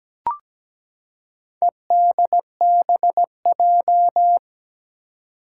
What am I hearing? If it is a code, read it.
Morse code, heard as a keyed tone, about 17 wpm, 700 Hz. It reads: EDBJ
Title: Four Challenging Letter-Number Combinations 17wpm